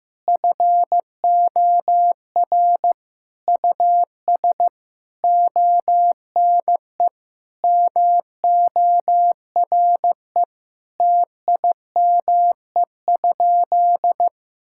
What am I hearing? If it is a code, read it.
Morse code, heard as a keyed tone, about 15 wpm, 700 Hz. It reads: FOR US ONE MORE TIME?
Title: Sentences from Top 1000 Words - Review Part 5 - 15wpm